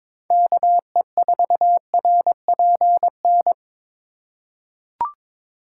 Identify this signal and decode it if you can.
Morse code, heard as a keyed tone, about 22 wpm, 700 Hz. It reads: KE4RPN
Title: USA Call Signs - 6 Characters - 3x Character Spacing - 22wpm